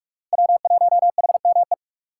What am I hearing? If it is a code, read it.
Morse code, heard as a keyed tone, about 44 wpm, 700 Hz. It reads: W1HME